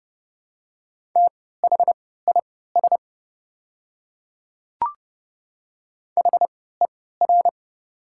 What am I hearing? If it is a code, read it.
Morse code, heard as a keyed tone, about 30 wpm, spaced as for 17 wpm, 700 Hz. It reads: THIS HER
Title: 3x Character Spacing - Top 100 Words - 30wpm